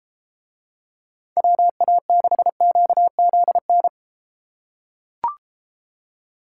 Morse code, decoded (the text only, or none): WA6QZN